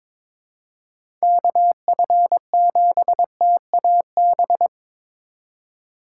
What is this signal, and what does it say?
Morse code, heard as a keyed tone, about 22 wpm, 700 Hz. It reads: KF7TAB
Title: USA Call Signs - 6 Characters - 4x Character Spacing - 22wpm